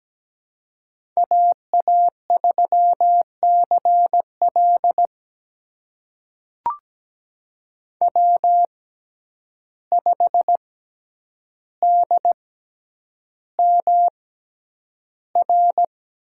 Morse code, heard as a keyed tone, about 17 wpm, spaced as for 9 wpm, 700 Hz. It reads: AA3CL W5DMR
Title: USA Call Signs - 5 Characters - 6x Character Spacing - 17wpm